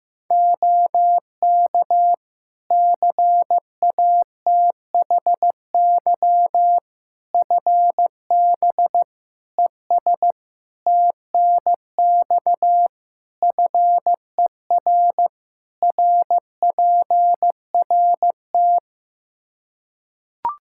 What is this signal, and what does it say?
Morse code, heard as a keyed tone, about 15 wpm, 700 Hz. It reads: OK CATHY FB ES TNX FER RPRT
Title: K1USN SST QSO Practice 15wpm